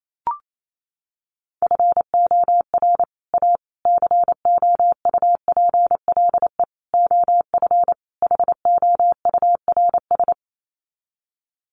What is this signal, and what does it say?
Morse code, heard as a keyed tone, about 28 wpm, 700 Hz. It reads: FOR A COUPLE OF HOURS
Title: Sets of 5 Words Encore2 28wpm